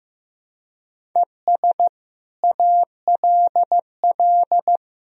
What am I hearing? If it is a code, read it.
Morse code, heard as a keyed tone, about 15 wpm, 700 Hz. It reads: ES ALL
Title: Sentences from Top 100 Words and Abbreviations 15wpm